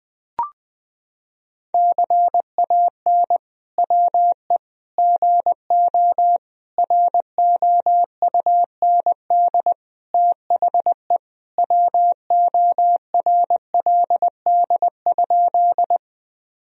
Morse code, heard as a keyed tone, about 21 wpm, 700 Hz. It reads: CAN WE GO ROUND THE WORLD?
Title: Sentences from Top 1000 Words - Review Part 1 - 20wpm